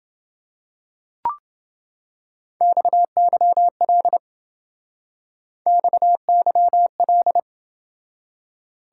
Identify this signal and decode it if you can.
Morse code, heard as a keyed tone, about 29 wpm, 700 Hz. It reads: XYL XYL